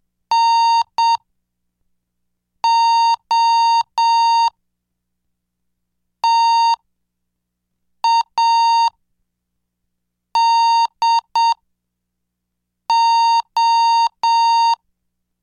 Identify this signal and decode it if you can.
Morse code, heard as a keyed tone, about 7 wpm, spaced as for 4 wpm, 920 Hz. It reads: NOTADO